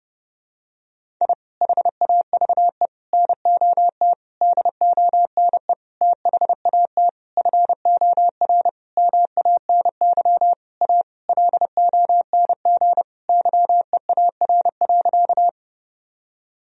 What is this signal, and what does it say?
Morse code, heard as a keyed tone, about 32 wpm, 700 Hz. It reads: I HAVE NOT DONE THAT FOR MANY A LONG YEAR.
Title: Sentences from Top 1000 Words - Review Part 1 - 30wpm